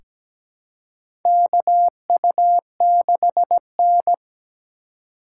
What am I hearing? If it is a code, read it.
Morse code, heard as a keyed tone, about 17 wpm, 700 Hz. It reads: KU6N
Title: USA Call Signs - 4 Characters - 2x Character Spacing - 17wpm